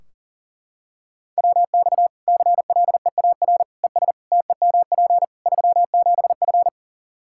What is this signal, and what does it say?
Morse code, heard as a keyed tone, about 40 wpm, 700 Hz. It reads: WX CLEAR ES TEMP 37F
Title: K1USN SST QSO Practice 40wpm